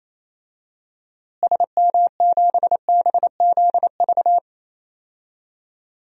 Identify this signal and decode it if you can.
Morse code, heard as a keyed tone, about 28 wpm, 700 Hz. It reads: SM7BZV
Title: DXCC-100 Call Signs - 6 Characters - Repeated 3x - 28wpm